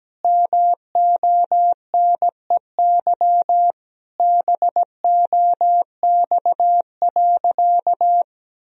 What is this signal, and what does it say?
Morse code, heard as a keyed tone, about 17 wpm, 700 Hz. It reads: MONEY BOX.
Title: Sentences from Top 1000 Words - Review Part 2 - 17wpm